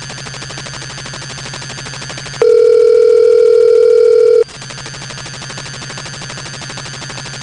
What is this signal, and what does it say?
Ringback tone (North America), 440 + 480 Hz.